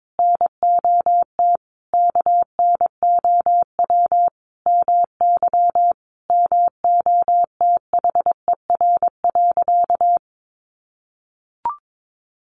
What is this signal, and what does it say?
Morse code, heard as a keyed tone, about 22 wpm, 700 Hz. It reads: NOT KNOW MY MOTHER.
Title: Sentences from Top 600 Words 22wpm